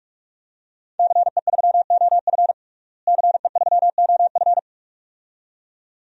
Morse code, heard as a keyed tone, about 42 wpm, 700 Hz. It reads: CE3OF CE3OF
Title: DXCC-100 Call Signs - 5 Characters - Repeated 2x - 45wpm